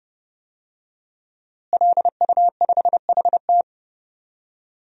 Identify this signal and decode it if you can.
Morse code, heard as a keyed tone, about 30 wpm, 700 Hz. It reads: LU5HT
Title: DXCC-100 Call Signs - 5 Characters - Repeated 2x - 30wpm